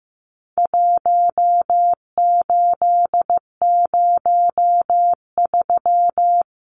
Morse code, heard as a keyed tone, about 15 wpm, 700 Hz. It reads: 1803
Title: Numbers - 4-Digit 15wpm